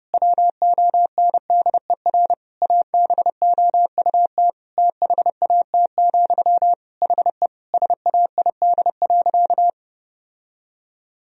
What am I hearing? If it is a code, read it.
Morse code, heard as a keyed tone, about 32 wpm, 700 Hz. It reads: WONDER ABOUT THAT, HE SAID.
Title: Sentences from Top 1000 Words - Review Part 3 - 30wpm